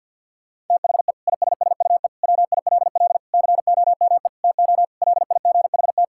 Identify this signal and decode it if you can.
Morse code, heard as a keyed tone, about 50 wpm, 700 Hz. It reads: THE ISSUE WILL COME TO LIGHT
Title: Sentences from Top 1000 Words - Review Part 6 - 50wpm